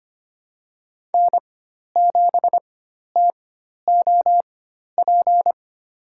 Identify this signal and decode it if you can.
Morse code, heard as a keyed tone, about 25 wpm, spaced as for 12 wpm, 700 Hz. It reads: N7TOP
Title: USA Call Signs - 5 Characters - 4x Character Spacing - 25wpm